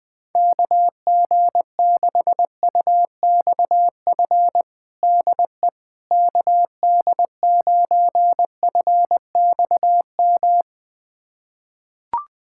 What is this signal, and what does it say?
Morse code, heard as a keyed tone, about 21 wpm, 700 Hz. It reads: KG6UXF DE KD9FXM
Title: USA Callsign DE Callsign 20wpm